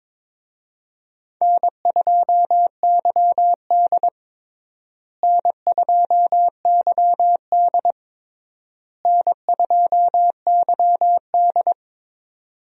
Morse code, heard as a keyed tone, about 21 wpm, 700 Hz. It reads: N2YD N2YD N2YD
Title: USA Call Signs - 4 Characters - Repeated 3x - 22wpm